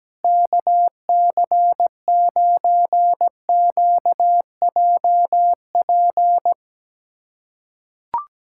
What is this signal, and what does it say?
Morse code, heard as a keyed tone, about 17 wpm, 700 Hz. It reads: KC9QJP
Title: USA Callsign DE Callsign - 6 Characters - 17wpm